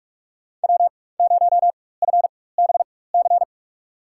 Morse code, heard as a keyed tone, about 46 wpm, spaced as for 21 wpm, 700 Hz. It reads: W0FBC
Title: USA Call Signs - 4x Character Spacing - 45wpm